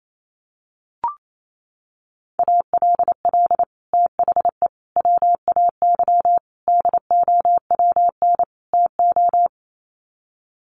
Morse code, heard as a keyed tone, about 28 wpm, 700 Hz. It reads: ALL THE WAY DOWN TO
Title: Sets of 5 Words Encore2 28wpm